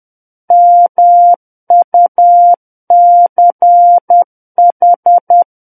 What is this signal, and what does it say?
Morse code, heard as a keyed tone, about 10 wpm, 700 Hz. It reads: MUCH
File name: Book One / 01-Chapter 1 - 10 WPM